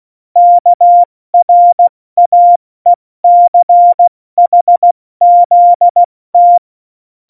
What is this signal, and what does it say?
Morse code, heard as a keyed tone, about 16 wpm, 700 Hz. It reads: KRAECHZT
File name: Morsekurs / Witze / Witz 04. Dezember, 70 F80